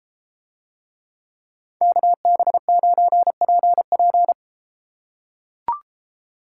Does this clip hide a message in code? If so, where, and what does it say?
Yes: Morse code from 1.8 to 4.3 s, heard as a keyed tone, about 33 wpm, 700 Hz. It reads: KB9PP